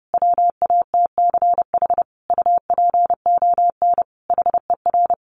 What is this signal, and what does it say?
Morse code, heard as a keyed tone, about 30 wpm, 700 Hz. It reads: WATCH UPON HER
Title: Sentences from Top 1100 Words 30wpm